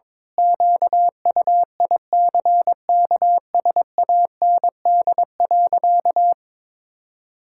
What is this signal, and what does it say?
Morse code, heard as a keyed tone, about 22 wpm, 700 Hz. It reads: QUICKSAND.
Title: Pangram Challenge v2 - 22wpm